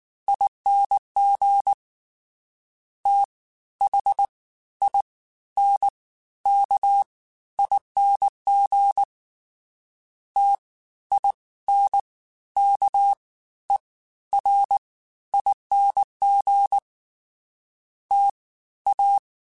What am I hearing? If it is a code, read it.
Morse code, heard as a keyed tone, about 19 wpm, spaced as for 13 wpm, 800 Hz. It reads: ING THINKING TINKERING TA
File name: Special INGs-mp3s / Special INGs-mp3s / Fast ing @18